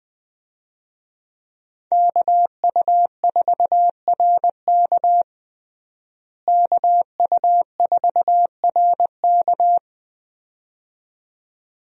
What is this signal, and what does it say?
Morse code, heard as a keyed tone, about 21 wpm, 700 Hz. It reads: KU4RK KU4RK
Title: USA Call Signs - 5 Characters - Repeated 2x - 20wpm